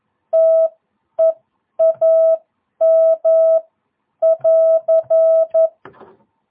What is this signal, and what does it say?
Morse code, heard as a keyed tone, about 11 wpm, spaced as for 9 wpm, 660 Hz. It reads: TEAM<AR>
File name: jh8cbh / 2022cw